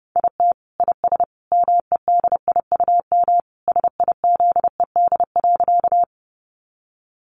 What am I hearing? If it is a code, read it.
Morse code, heard as a keyed tone, about 30 wpm, 700 Hz. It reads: IT IS MEDIUM SIZED.
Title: Easy Phrases 30wpm